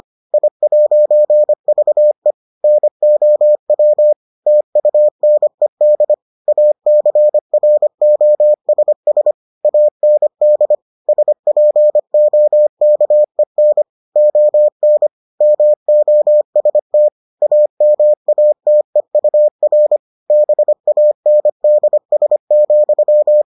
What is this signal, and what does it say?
Morse code, heard as a keyed tone, about 25 wpm, 600 Hz. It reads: I'VE NOW TUNED ACROSS AND SPOKEN ON MOST AMATEUR BANDS,